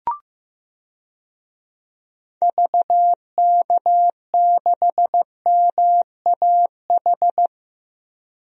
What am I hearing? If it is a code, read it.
Morse code, heard as a keyed tone, about 15 wpm, 700 Hz. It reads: VK6MAH
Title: DXCC-100 Call Signs - 6 Characters - 15wpm